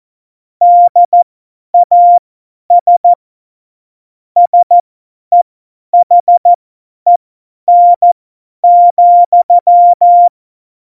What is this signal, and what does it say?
Morse code, heard as a keyed tone, about 14 wpm, spaced as for 10 wpm, 700 Hz. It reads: DAS SEHEN,
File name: Morsekurs / Witze / Witz 01. Dezember 2020, 50 F70